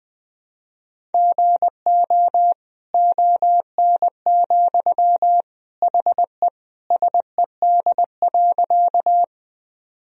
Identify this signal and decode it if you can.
Morse code, heard as a keyed tone, about 20 wpm, 700 Hz. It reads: GO ON, HE SED.
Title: Sentences from Top 100 Words and Abbreviations 20wpm